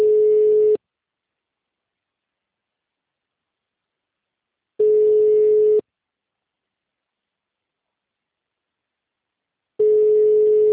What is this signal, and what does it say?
Ringback tone (Europe, 425 Hz).
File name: out- -2049- - - .57349